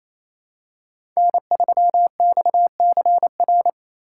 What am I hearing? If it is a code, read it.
Morse code, heard as a keyed tone, about 28 wpm, 700 Hz. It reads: N3XCR
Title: USA Call Signs - 5 Characters - 4x Character Spacing - 28wpm